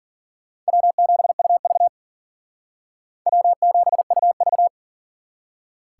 Morse code, heard as a keyed tone, about 51 wpm, 690 Hz. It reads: W7UV W7UV